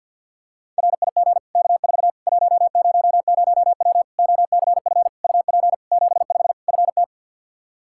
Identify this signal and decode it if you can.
Morse code, heard as a keyed tone, about 51 wpm, 700 Hz. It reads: RIG K4 100W OCF UP 75 FT
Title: Practice QSO - Exchange 2 - Info - 50wpm